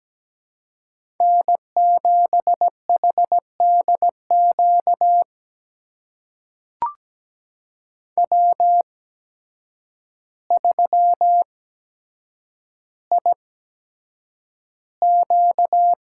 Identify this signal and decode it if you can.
Morse code, heard as a keyed tone, about 17 wpm, spaced as for 8 wpm, 700 Hz. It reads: N7HDQ W3IQ
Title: USA Call Signs - 8x Character Spacing - 17wpm